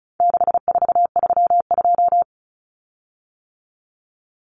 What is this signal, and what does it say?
Morse code, heard as a keyed tone, about 34 wpm, 700 Hz. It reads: 6432
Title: Common Cut Numbers - 4-Digit 35wpm